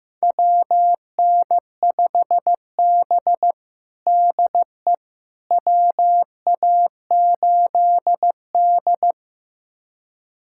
Morse code, heard as a keyed tone, about 16 wpm, 700 Hz. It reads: WN5B DE WA8D
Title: USA Callsign DE Callsign - 4 Characters - 15wpm